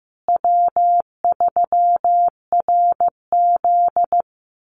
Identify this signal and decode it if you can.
Morse code, heard as a keyed tone, about 15 wpm, 700 Hz. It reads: W3RZ
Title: Easy Call Signs Repeated 2x - 15wpm